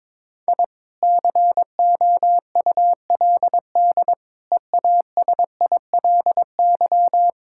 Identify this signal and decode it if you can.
Morse code, heard as a keyed tone, about 22 wpm, 700 Hz. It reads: I COULD EASILY
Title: Sentences from Top 1000 Words - Review Part 6 - 22wpm